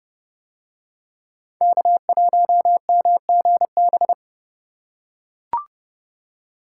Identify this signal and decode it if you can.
Morse code, heard as a keyed tone, about 27 wpm, 700 Hz. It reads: K1MGB